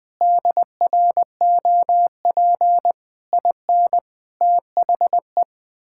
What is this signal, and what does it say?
Morse code, heard as a keyed tone, about 20 wpm, 700 Hz. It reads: DROP IN THE
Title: Sentences with 7-Word Phrases - 20wpm